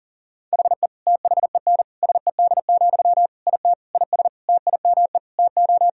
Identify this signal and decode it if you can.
Morse code, heard as a keyed tone, about 40 wpm, 700 Hz. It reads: HE THEN SED, IT IS TIME TO
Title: Sentences from Top 300 Words and Abbreviations 40wpm